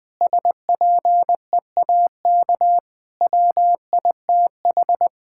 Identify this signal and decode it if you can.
Morse code, heard as a keyed tone, about 20 wpm, 700 Hz. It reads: SPEAK WITH
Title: Sentences from Top 1600 Words 20wpm